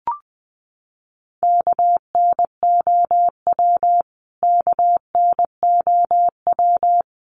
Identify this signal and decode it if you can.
Morse code, heard as a keyed tone, about 20 wpm, 700 Hz. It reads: KNOW KNOW
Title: Top 200 Words Repeated 20wpm